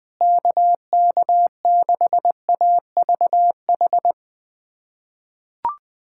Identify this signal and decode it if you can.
Morse code, heard as a keyed tone, about 20 wpm, 700 Hz. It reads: KK6AVH
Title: USA Callsign DE Callsign - 6 Characters - 20wpm